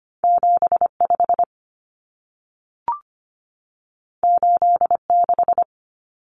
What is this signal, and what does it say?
Morse code, heard as a keyed tone, about 25 wpm, 700 Hz. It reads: 75 86
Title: Numbers - 2-Digit 25wpm